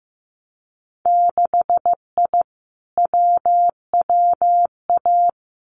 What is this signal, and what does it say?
Morse code, heard as a keyed tone, about 15 wpm, 700 Hz. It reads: 6I WWA
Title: Winter Field Day Exchanges Encore 15wpm